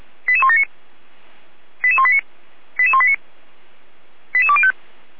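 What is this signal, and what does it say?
Five-tone selective call (ZVEI), heard as quick runs of five tones, tones of about 70 ms. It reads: 88179 88189 88189 88205
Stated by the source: ZVEI five-tone selective call received on an SDR, from the Feuerwehr Orlamünde fire brigade.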